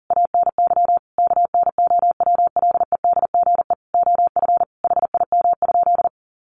Morse code, heard as a keyed tone, about 40 wpm, 700 Hz. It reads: ANY KNOWLEDGE OF HIM?
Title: Sentences from Top 600 Words 40wpm